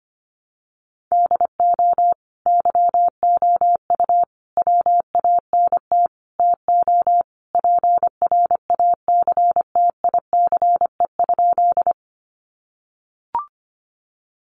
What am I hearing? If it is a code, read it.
Morse code, heard as a keyed tone, about 25 wpm, 700 Hz. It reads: DO YOU WANT TO PRACTICE?
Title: Sentences from Top 700 Words 25wpm